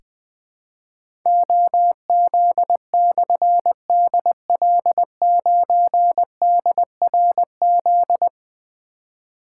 Morse code, heard as a keyed tone, about 20 wpm, 700 Hz. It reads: OZ/DL9DRZ
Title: Difficult Call Signs 20wpm